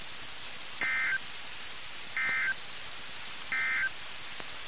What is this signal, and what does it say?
Emergency Alert System (SAME) tones, heard as harsh buzzing bursts. It is heard from 0.7 to 4.0 s.
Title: NOAA Weather Radio - KIH61